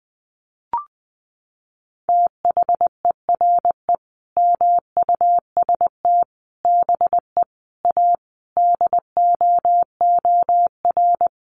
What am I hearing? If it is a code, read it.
Morse code, heard as a keyed tone, about 20 wpm, 700 Hz. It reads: THERE MUST BE A DOOR
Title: Sentences from Top 1400 Words 20wpm